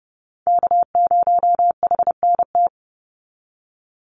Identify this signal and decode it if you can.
Morse code, heard as a keyed tone, about 30 wpm, 700 Hz. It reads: K0HNT
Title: USA Call Signs 30wpm